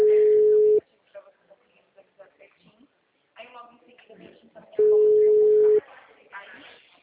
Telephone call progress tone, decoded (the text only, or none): ringback tone